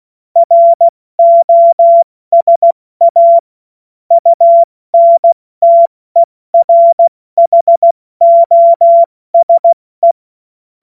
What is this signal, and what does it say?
Morse code, heard as a keyed tone, about 16 wpm, 680 Hz. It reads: ROSA UNTERHOSE
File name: Morsekurs / Witze / Witz 10. August 2021, 70 F80